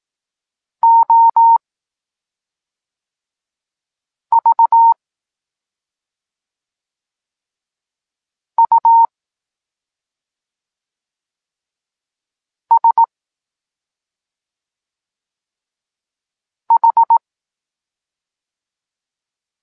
Morse code, heard as a keyed tone, about 18 wpm, spaced as for 3 wpm, 920 Hz. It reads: OVUSH
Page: Tekst 8 – doskonalenie odbioru poznanych liter